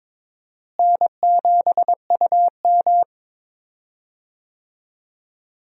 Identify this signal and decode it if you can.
Morse code, heard as a keyed tone, about 21 wpm, 700 Hz. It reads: N7UM
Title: USA Call Signs - 4 Characters - Repeated 3x - 22wpm